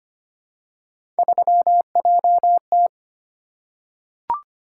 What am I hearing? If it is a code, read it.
Morse code, heard as a keyed tone, about 25 wpm, 700 Hz. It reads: 3JT
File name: Three Letter-Number Combinations - 25wpm